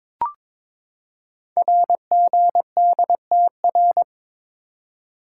Morse code, heard as a keyed tone, about 22 wpm, 700 Hz. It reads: RGDTR
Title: Code Groups - 5 Letters - 22wpm